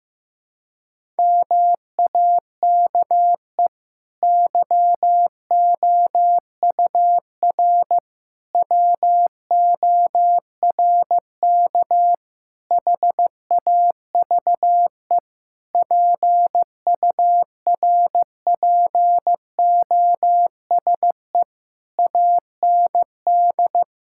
Morse code, heard as a keyed tone, about 16 wpm, 700 Hz. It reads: MAKE YOUR WORK HAVE PURPOSE AND
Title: Sentences from Top 1000 Words - Review Part 6 - 15wpm